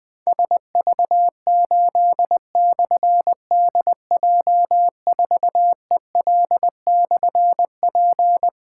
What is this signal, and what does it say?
Morse code, heard as a keyed tone, about 20 wpm, 700 Hz. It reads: SV8/DJ4EL/P
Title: Difficult Call Signs 20wpm